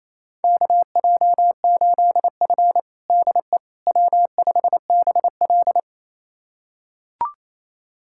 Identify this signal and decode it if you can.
Morse code, heard as a keyed tone, about 28 wpm, 700 Hz. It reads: KJ8F DE W5BL
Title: USA Callsign DE Callsign - 4 Characters - 28wpm